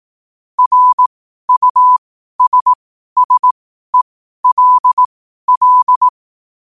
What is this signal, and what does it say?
Morse code, heard as a keyed tone, about 18 wpm, spaced as for 12 wpm, 1000 Hz. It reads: RUSSELL
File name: qotdfiles / 12wpm